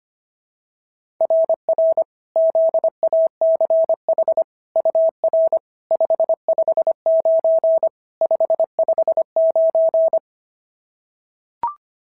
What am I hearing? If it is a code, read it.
Morse code, heard as a keyed tone, about 25 wpm, 690 Hz. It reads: RR ZACH UR 559 559
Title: SOTA QSO Practice 25wpm